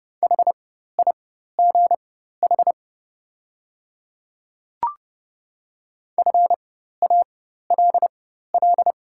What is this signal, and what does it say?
Morse code, heard as a keyed tone, about 32 wpm, spaced as for 14 wpm, 690 Hz. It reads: HIGH FALL
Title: 4x Character Spacing - Top 500 Words - 30wpm